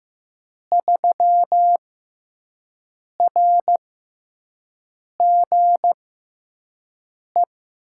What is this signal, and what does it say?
Morse code, heard as a keyed tone, about 15 wpm, spaced as for 5 wpm, 700 Hz. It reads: 3RGE
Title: USA Call Signs - 6 Characters - 6x Character Spacing - 15wpm